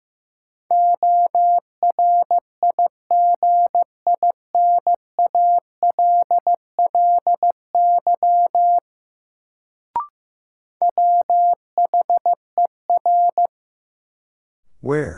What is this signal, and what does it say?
Morse code, heard as a keyed tone, about 15 wpm, 700 Hz. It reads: ORIGINALLY WHER
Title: Sentences from Top 1500 Words - New Word Drill Alt - 15wpm